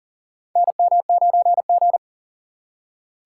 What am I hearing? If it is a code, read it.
Morse code, heard as a keyed tone, about 40 wpm, 700 Hz. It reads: NM9G
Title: USA Call Signs - 4 Characters - 6x Character Spacing - 40wpm